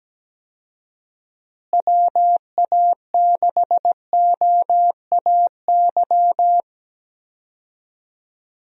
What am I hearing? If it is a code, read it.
Morse code, heard as a keyed tone, about 17 wpm, 700 Hz. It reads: WA6OAY
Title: USA Call Signs - 6 Characters - Repeated 2x - 17wpm